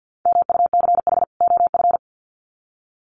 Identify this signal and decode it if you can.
Morse code, heard as a keyed tone, about 50 wpm, 700 Hz. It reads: MUCH OF
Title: Sets of 2 Words 50wpm